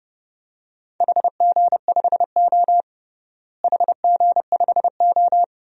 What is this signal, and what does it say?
Morse code, heard as a keyed tone, about 31 wpm, 700 Hz. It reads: HG5O HG5O
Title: DXCC-100 Call Signs - 4 Characters - Repeated 2x - 30wpm